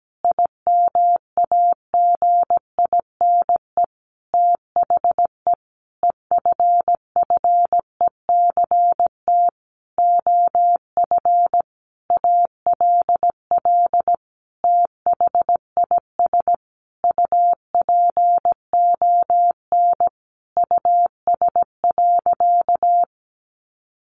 Morse code, heard as a keyed tone, about 17 wpm, 700 Hz. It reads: IMAGINE THE EFFECT OF ALL THIS UPON US.
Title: Sentences from Top 1400 Words 17wpm